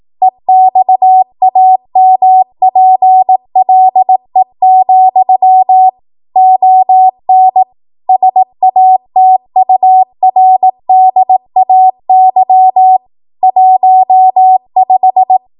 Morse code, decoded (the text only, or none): EXAMPLE, ON SATURDAY 15